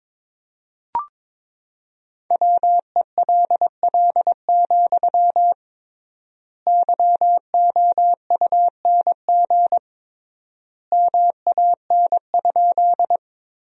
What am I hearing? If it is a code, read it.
Morse code, heard as a keyed tone, about 22 wpm, 700 Hz. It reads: WELL, YOUNG MAN?
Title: 3x Word-Spacing - Sentences from Top 400 Words - 22wpm